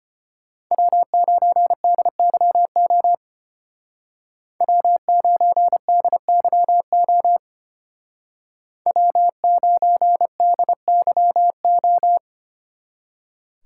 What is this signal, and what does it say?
Morse code, heard as a keyed tone, about 31 wpm, 700 Hz. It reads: W9DYO W9DYO W9DYO
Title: Speed Racing - USA 1x3 Call Signs 25wpm